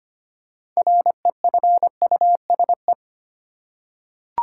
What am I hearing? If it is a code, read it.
Morse code, heard as a keyed tone, about 25 wpm, 700 Hz. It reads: REFUSE
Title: Six Letter Words - 25wpm